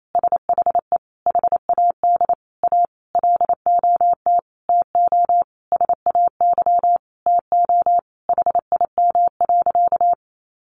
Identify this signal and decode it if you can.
Morse code, heard as a keyed tone, about 28 wpm, 700 Hz. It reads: SHE HAD A LOT TO SAY TO HIM.
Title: Sentences from Top 1000 Words 28wpm